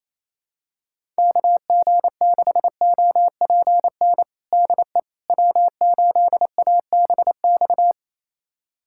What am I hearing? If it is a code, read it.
Morse code, heard as a keyed tone, about 28 wpm, 700 Hz. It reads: KG6OPN DE W8ABX
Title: USA Callsign DE Callsign 28wpm